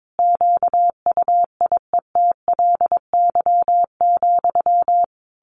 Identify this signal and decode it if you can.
Morse code, heard as a keyed tone, about 22 wpm, 700 Hz. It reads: QUIETLY,
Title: A Sound Of Thunder 22wpm